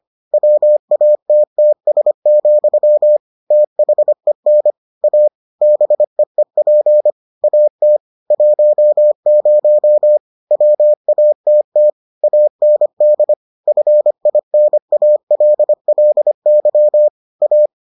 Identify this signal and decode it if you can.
Morse code, heard as a keyed tone, about 25 wpm, 600 Hz. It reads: WATTS, THEN A BEEP AT 10 WATT AND FINALLY A